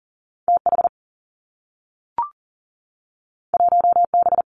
Morse code, heard as a keyed tone, about 39 wpm, 700 Hz. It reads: T H 1 B
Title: Two Challenging Letter-Number Combinations 40wpm